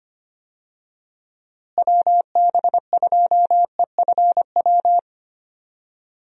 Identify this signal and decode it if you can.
Morse code, heard as a keyed tone, about 25 wpm, 700 Hz. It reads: WB2EFW
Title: USA Call Signs - 6 Characters - Repeated 3x - 25wpm